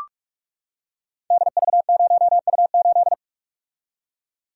Morse code, heard as a keyed tone, about 44 wpm, 700 Hz. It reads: DV0U8
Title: Code Groups - 5 Letter:Numbers - 45wpm